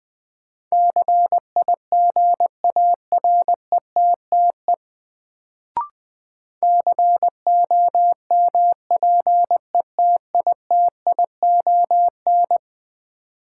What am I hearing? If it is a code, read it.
Morse code, heard as a keyed tone, about 20 wpm, 700 Hz. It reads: CIGARETTE COMPETITION
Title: Sentences from Top 1000 Words - New Word Drill - 20wpm